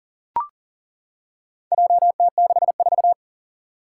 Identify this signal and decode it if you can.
Morse code, heard as a keyed tone, about 37 wpm, 700 Hz. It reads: JT64
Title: Four Challenging Letter-Number Combinations 40wpm